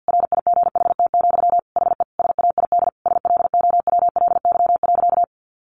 Morse code, heard as a keyed tone, about 50 wpm, 700 Hz. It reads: RIGHT, HE SAID SLOWLY.
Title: Sentences from Top 1100 Words 50wpm